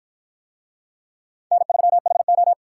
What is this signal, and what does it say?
Morse code, heard as a keyed tone, about 48 wpm, 700 Hz. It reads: N3HO